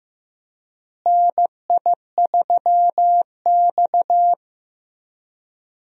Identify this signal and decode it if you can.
Morse code, heard as a keyed tone, about 15 wpm, 700 Hz. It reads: NI3X
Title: USA Call Signs - 4 Characters - Repeated 2x - 15wpm